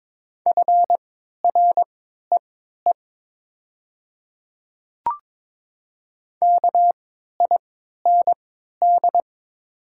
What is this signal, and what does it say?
Morse code, heard as a keyed tone, about 22 wpm, spaced as for 12 wpm, 700 Hz. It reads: FREE KIND
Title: 3x Character Spacing - Top 500 Words - 22wpm